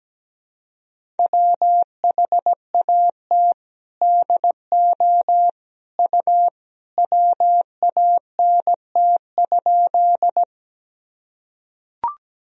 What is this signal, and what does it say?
Morse code, heard as a keyed tone, about 17 wpm, 700 Hz. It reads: WHAT DO U WANT?
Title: Sentences from Top 200 Words and Abbreviations 17wpm